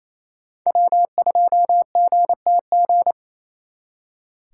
Morse code, heard as a keyed tone, about 32 wpm, 700 Hz. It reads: W2GTG
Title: Speed Racing - USA 1x3 Call Signs 28wpm